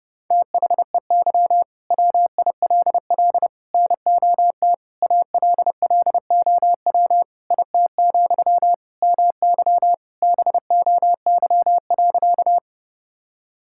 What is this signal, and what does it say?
Morse code, heard as a keyed tone, about 30 wpm, 700 Hz. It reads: THEY WILL NOT ALLOW IT, MY BOY.
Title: Sentences from Top 1000 Words - Review Part 5 - 30wpm